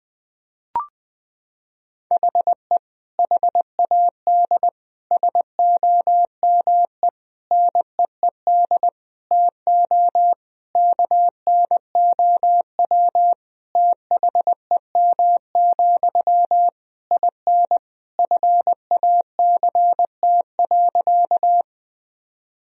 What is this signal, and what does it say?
Morse code, heard as a keyed tone, about 20 wpm, 700 Hz. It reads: HE HAD SOME NEED TO KNOW THEM, IN FACT.
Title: Sentences from Top 1000 Words - Review Part 3 - 20wpm